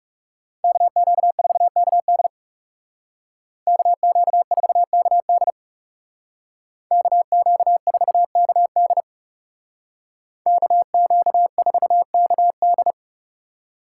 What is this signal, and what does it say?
Morse code, heard as a keyed tone, about 35 wpm, 700 Hz. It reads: KQ4KD KQ4KD KQ4KD KQ4KD
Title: Speed Racing - USA 2x2 Call Signs 30wpm